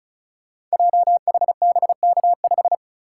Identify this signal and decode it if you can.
Morse code, heard as a keyed tone, about 36 wpm, 700 Hz. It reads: JHBK5
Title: Code Groups - 5 Letter:Numbers - 35wpm